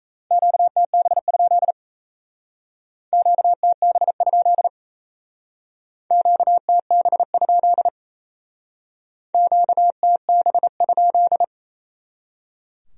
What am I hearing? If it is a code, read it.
Morse code, heard as a keyed tone, about 36 wpm, 700 Hz. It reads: QTB? QTB? QTB? QTB?